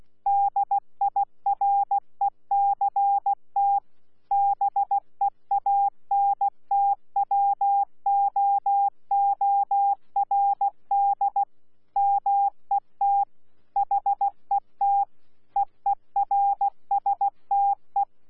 Morse code, heard as a keyed tone, about 16 wpm, 800 Hz. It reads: DIRECT BEANTWOORD MET HET EERSTE
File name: Z PI4VRZ / PI4VRZ 14-jun-2025